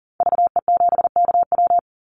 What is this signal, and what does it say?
Morse code, heard as a keyed tone, about 40 wpm, 700 Hz. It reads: VE7KW
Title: Call Signs - 40wpm